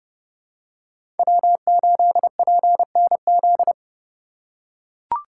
Morse code, heard as a keyed tone, about 29 wpm, 700 Hz. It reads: W8PNZ
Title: USA Call Signs - 5 Characters - 2x Character Spacing - 30wpm